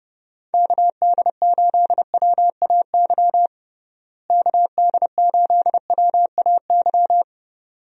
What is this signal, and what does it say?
Morse code, heard as a keyed tone, about 31 wpm, 700 Hz. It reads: KD8WAY KD8WAY
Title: USA Call Signs - 6 Characters - Repeated 2x - 30wpm